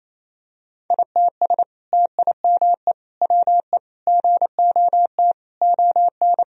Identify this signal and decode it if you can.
Morse code, heard as a keyed tone, about 28 wpm, 700 Hz. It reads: ITS TIME WE GOT ON
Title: Sentences from Top 1000 Words - Review Part 4 - 28wpm